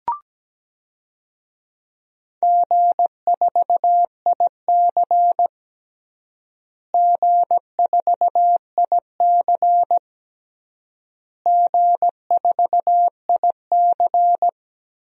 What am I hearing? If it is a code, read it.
Morse code, heard as a keyed tone, about 17 wpm, 700 Hz. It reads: G4IC G4IC G4IC
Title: DXCC-100 Call Signs - 4 Characters - Repeated 3x - 17wpm